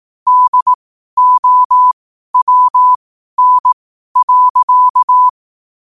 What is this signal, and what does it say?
Morse code, heard as a keyed tone, about 18 wpm, spaced as for 12 wpm, 1000 Hz. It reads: DOWN.
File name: qotdfiles / 12wpm